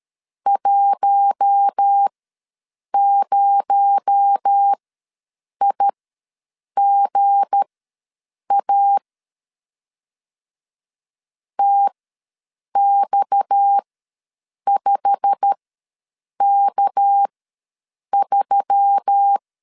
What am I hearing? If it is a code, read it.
Morse code, heard as a keyed tone, about 13 wpm, spaced as for 7 wpm, 800 Hz. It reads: 10IGA TX5K3
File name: LEZIONI / 9 286 ws 12wpm 3351